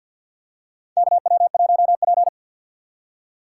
Morse code, heard as a keyed tone, about 47 wpm, 700 Hz. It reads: KW1P
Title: USA Call Signs - 4 Characters - 6x Character Spacing - 50wpm